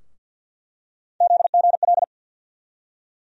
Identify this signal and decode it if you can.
Morse code, heard as a keyed tone, about 51 wpm, 700 Hz. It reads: ZGL
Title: USA Call Sign Postfixes - 3 Letters 50wpm